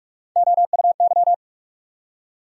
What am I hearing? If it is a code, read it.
Morse code, heard as a keyed tone, about 41 wpm, 700 Hz. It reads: OUY
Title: USA Call Sign Postfixes - 3 Letters 45wpm